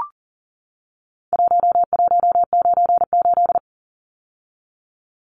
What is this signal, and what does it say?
Morse code, heard as a keyed tone, about 40 wpm, 700 Hz. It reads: 1198
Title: Numbers - Contest Style - 4-Digit 40wpm